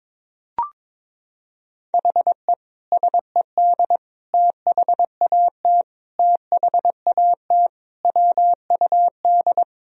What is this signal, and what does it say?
Morse code, heard as a keyed tone, about 22 wpm, 700 Hz. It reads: HE SED THAT THAT WUD
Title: Sentences from Top 200 Words and Abbreviations 22wpm